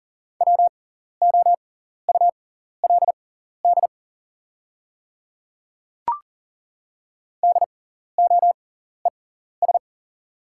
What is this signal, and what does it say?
Morse code, heard as a keyed tone, about 40 wpm, spaced as for 14 wpm, 690 Hz. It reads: WOULD DOES
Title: 6x Character Spacing - Top 300 Words - 40wpm